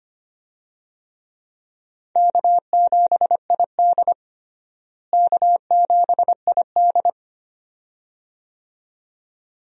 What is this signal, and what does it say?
Morse code, heard as a keyed tone, about 25 wpm, 700 Hz. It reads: K7ID K7ID
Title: USA Call Signs - 4 Characters - Repeated 2x - 25wpm